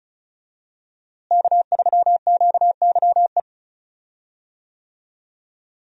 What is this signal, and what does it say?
Morse code, heard as a keyed tone, about 36 wpm, 700 Hz. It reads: K3QYE
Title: USA Call Signs - 5 Characters - Repeated 3x - 35wpm